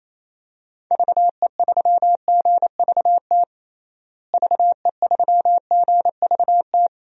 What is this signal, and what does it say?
Morse code, heard as a keyed tone, about 28 wpm, 700 Hz. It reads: VE3GVT VE3GVT
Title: DXCC-100 Call Signs - 6 Characters - Repeated 3x - 28wpm